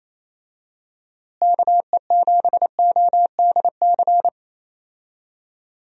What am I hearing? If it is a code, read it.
Morse code, heard as a keyed tone, about 28 wpm, 700 Hz. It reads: KE7ODC
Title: USA Call Signs - 6 Characters - Repeated 3x - 28wpm